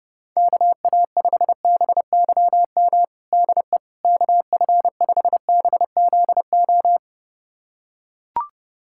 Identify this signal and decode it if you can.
Morse code, heard as a keyed tone, about 30 wpm, 700 Hz. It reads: KA5BYM DE KF5BZO
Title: USA Callsign DE Callsign 30wpm